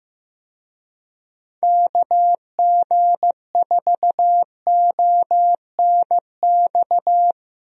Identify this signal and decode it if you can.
Morse code, heard as a keyed tone, about 15 wpm, 700 Hz. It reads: KG4ONX
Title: USA Call Signs - 6 Characters - Repeated 2x - 15wpm